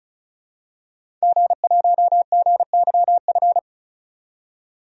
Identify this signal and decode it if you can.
Morse code, heard as a keyed tone, about 36 wpm, 700 Hz. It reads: G1GYF
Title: DXCC-100 Call Signs - 5 Characters - 35wpm